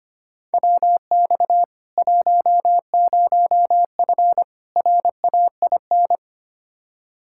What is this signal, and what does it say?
Morse code, heard as a keyed tone, about 25 wpm, 700 Hz. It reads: WX 10F RAIN
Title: K1USN SST QSO Practice 25wpm